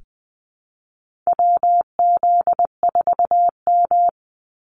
Morse code, heard as a keyed tone, about 20 wpm, 700 Hz. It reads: WZ4M
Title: NA SOTA Activator Callsigns 20wpm